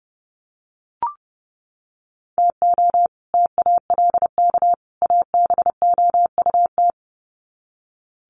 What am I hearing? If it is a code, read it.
Morse code, heard as a keyed tone, about 30 wpm, 700 Hz. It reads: TO TALK ABOUT
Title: Sets of 3 Words 30wpm